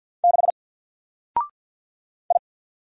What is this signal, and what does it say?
Morse code, heard as a keyed tone, about 50 wpm, 700 Hz. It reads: 6 I
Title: Single Letter-Number Alt 50wpm